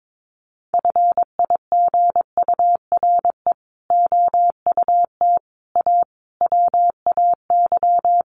Morse code, heard as a keyed tone, about 22 wpm, 700 Hz. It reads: FIGURE OUT A WAY
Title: Sets of 5 Words Encore2 22wpm